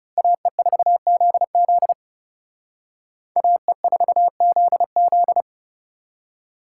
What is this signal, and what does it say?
Morse code, heard as a keyed tone, about 38 wpm, 700 Hz. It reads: AE4ZZ AE4ZZ